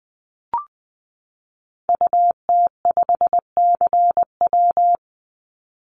Morse code, heard as a keyed tone, about 20 wpm, 700 Hz. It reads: UT5CW
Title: Call Signs Repeated 2x - 20wpm